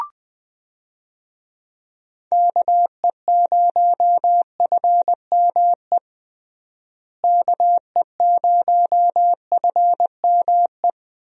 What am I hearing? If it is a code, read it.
Morse code, heard as a keyed tone, about 20 wpm, 700 Hz. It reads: KE0FME KE0FME
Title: USA Call Signs - 6 Characters - Repeated 3x - 20wpm